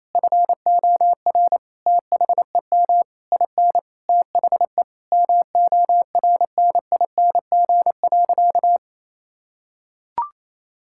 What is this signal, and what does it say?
Morse code, heard as a keyed tone, about 28 wpm, 700 Hz. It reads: FOR THEM IN THE MORNING.
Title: Sentences from Top 1000 Words - Review Part 5 - 28wpm